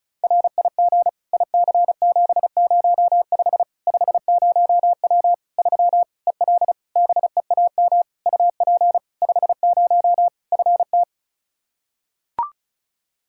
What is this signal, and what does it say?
Morse code, heard as a keyed tone, about 36 wpm, 700 Hz. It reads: RIG IC705 50W 3 EL BEAM UP 50 FT
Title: Practice QSO - Exchange 2 - Info - 35wpm